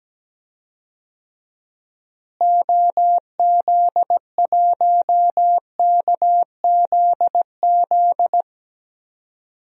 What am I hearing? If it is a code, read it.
Morse code, heard as a keyed tone, about 17 wpm, 700 Hz. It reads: OZ1KZZ
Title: DXCC-100 Call Signs - 6 Characters - Repeated 2x - 17wpm